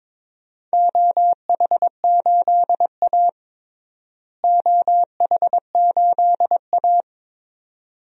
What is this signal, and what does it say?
Morse code, heard as a keyed tone, about 21 wpm, 700 Hz. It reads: OH8A OH8A
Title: DXCC-100 Call Signs - 4 Characters - Repeated 2x - 22wpm